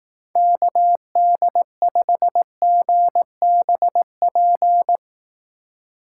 Morse code, heard as a keyed tone, about 18 wpm, 700 Hz. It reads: KD5GBP
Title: Speed Racing - USA 2x3 Call Signs 15wpm